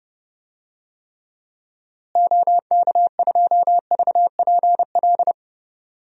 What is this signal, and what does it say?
Morse code, heard as a keyed tone, about 31 wpm, 700 Hz. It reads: OK2VPL
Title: DXCC-100 Call Signs - 6 Characters - Repeated 2x - 30wpm